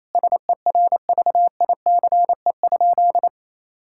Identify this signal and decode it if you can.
Morse code, heard as a keyed tone, about 27 wpm, 700 Hz. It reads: SERVICE?
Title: Sentences from Top 1000 Words - Review Part 5 - 28wpm